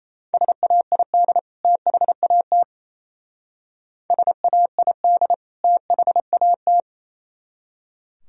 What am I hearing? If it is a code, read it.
Morse code, heard as a keyed tone, about 35 wpm, 700 Hz. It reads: SAID THAT SAID THAT
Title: Speed-Racing - Sets of 2 Words 28wpm